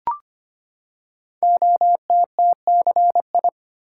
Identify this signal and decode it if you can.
Morse code, heard as a keyed tone, about 25 wpm, 700 Hz. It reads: OTTCI
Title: Code Groups - 5 Letters - 25wpm